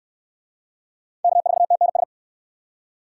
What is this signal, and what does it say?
Morse code, heard as a keyed tone, about 69 wpm, 700 Hz. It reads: K4TNS